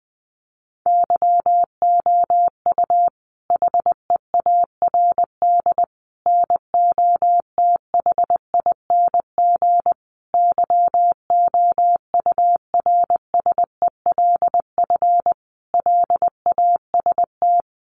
Morse code, heard as a keyed tone, about 20 wpm, 700 Hz. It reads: YOU HEARD NOTHING YOURSELF LAST
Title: Sentences from Top 600 Words 20wpm